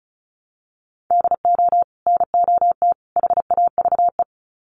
Morse code, heard as a keyed tone, about 34 wpm, 700 Hz. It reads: DO NOT HAVE
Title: Sets of 3 Words 35wpm